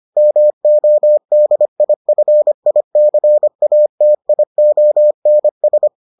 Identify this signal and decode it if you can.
Morse code, heard as a keyed tone, about 25 wpm, 600 Hz. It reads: MODIFICATIONS